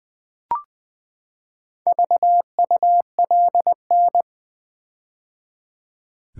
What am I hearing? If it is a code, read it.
Morse code, heard as a keyed tone, about 20 wpm, 700 Hz. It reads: VULN
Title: Build A Word - 4 Letters At A Time - 10-Letter Words - 20wpm